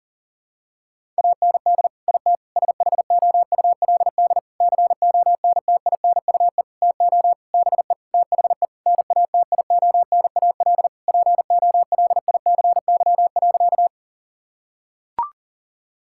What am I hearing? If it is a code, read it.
Morse code, heard as a keyed tone, about 40 wpm, 700 Hz. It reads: AND IT SHOULD CONTINUE TO BE THE NATIONAL POLICY.
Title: Sentences from Top 1000 Words - Review Part 6 - 40wpm